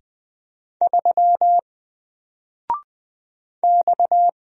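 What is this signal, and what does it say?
Morse code, heard as a keyed tone, about 20 wpm, 700 Hz. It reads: 3 X
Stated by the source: Single letter and number Morse drill at 20 wpm, from N4LCD.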